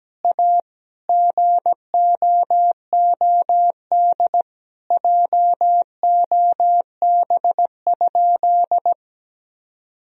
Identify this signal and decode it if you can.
Morse code, heard as a keyed tone, about 17 wpm, 700 Hz. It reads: A GOOD JOB?
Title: Sentences from Top 1000 Words - Review Part 3 - 17wpm